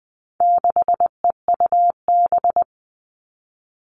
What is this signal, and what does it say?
Morse code, heard as a keyed tone, about 20 wpm, 700 Hz. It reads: 6EUB
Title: All Cut Numbers - 4-Digit 20wpm